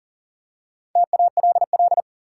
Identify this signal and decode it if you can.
Morse code, heard as a keyed tone, about 40 wpm, 700 Hz. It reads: TALL
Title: Four Letter Words - 40wpm